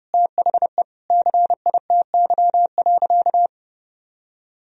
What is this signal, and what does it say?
Morse code, heard as a keyed tone, about 30 wpm, 700 Hz. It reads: THE CITY.
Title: Sentences from Top 1000 Words - Review Part 1 - 30wpm